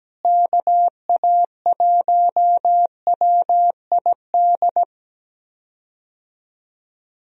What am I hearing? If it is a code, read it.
Morse code, heard as a keyed tone, about 17 wpm, 700 Hz. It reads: KA1WID
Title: USA Call Signs - 6 Characters - Repeated 3x - 17wpm